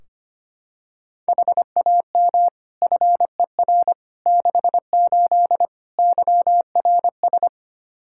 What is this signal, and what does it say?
Morse code, heard as a keyed tone, about 25 wpm, 700 Hz. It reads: HAM FER 68 YRS
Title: K1USN SST QSO Practice 25wpm